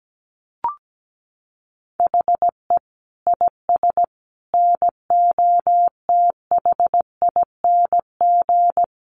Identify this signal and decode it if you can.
Morse code, heard as a keyed tone, about 17 wpm, 700 Hz. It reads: HE IS NOTHING
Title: Sentences from Top 1200 Words 17wpm